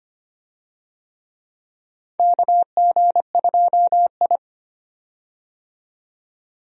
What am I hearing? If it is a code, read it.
Morse code, heard as a keyed tone, about 25 wpm, 700 Hz. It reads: KG2I
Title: USA Call Signs - 4 Characters - 25wpm